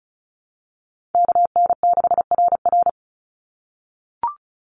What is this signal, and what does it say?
Morse code, heard as a keyed tone, about 34 wpm, 700 Hz. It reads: KN6RR
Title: USA Call Signs 35wpm